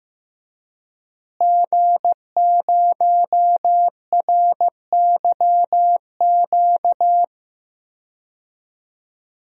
Morse code, heard as a keyed tone, about 15 wpm, 700 Hz. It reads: G0RYQ
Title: DXCC-100 Call Signs - 5 Characters - Repeated 3x - 15wpm